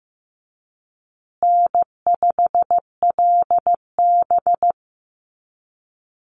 Morse code, heard as a keyed tone, about 15 wpm, 700 Hz. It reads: N5LB
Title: Easy Call Signs Repeated 3x - 15wpm